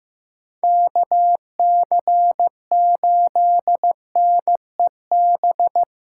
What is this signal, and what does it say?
Morse code, heard as a keyed tone, about 18 wpm, 700 Hz. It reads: KC8NEB
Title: Speed Racing - USA 2x3 Call Signs 15wpm